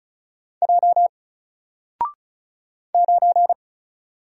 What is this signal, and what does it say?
Morse code, heard as a keyed tone, about 36 wpm, 700 Hz. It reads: J 9
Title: Single Letter-Number Alt 35wpm